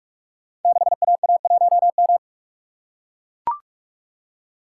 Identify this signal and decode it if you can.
Morse code, heard as a keyed tone, about 44 wpm, 700 Hz. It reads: 6AA1M
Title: Code Groups - 5 Letter:Numbers - 45wpm